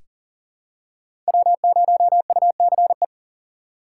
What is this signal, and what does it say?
Morse code, heard as a keyed tone, about 40 wpm, 700 Hz. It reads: W0UCE
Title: K1USN SST QSO Practice - 40wpm